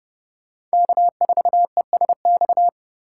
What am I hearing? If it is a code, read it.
Morse code, heard as a keyed tone, about 30 wpm, 700 Hz. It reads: K4ESX
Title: USA Call Signs - 3x Character Spacing - 30wpm